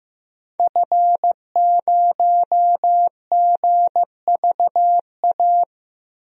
Morse code, heard as a keyed tone, about 15 wpm, 700 Hz. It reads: F0GVA
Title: DXCC-100 Call Signs - 5 Characters - 15wpm